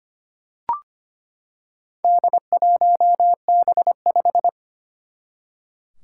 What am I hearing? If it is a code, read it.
Morse code, heard as a keyed tone, about 25 wpm, 700 Hz. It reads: D1B5
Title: Four Challenging Letter-Number Combinations 25wpm